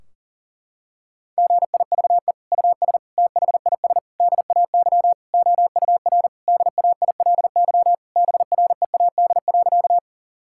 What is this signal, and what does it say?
Morse code, heard as a keyed tone, about 40 wpm, 700 Hz. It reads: GIVE US THIS DAY OUR DAILY BREAD.
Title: Sentences from Top 1600 Words 40wpm